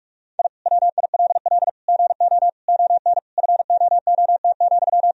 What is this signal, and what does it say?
Morse code, heard as a keyed tone, about 46 wpm, 700 Hz. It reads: I WILL GO ON FOOT,
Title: Sentences from Top 1000 Words - Review Part 3 - 45wpm